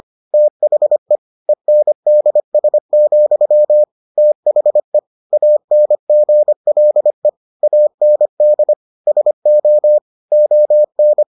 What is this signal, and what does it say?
Morse code, heard as a keyed tone, about 25 wpm, 600 Hz. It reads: THE ENDS, THE ANGLE AND SO ON